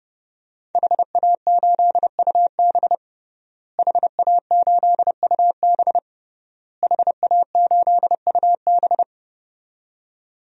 Morse code, heard as a keyed tone, about 30 wpm, 700 Hz. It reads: HA8UB HA8UB HA8UB
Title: DXCC-100 Call Signs - 5 Characters - Repeated 3x - 30wpm